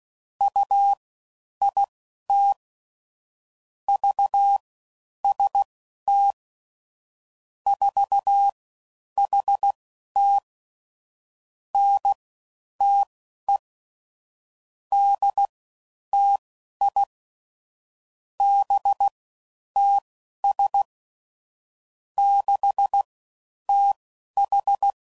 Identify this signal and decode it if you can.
Morse code, heard as a keyed tone, about 16 wpm, spaced as for 10 wpm, 800 Hz. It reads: UIT VST 4HT NTE DTI BTS 6TH
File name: Mixed / CW Comparisons / CW Comparisons Broken Characters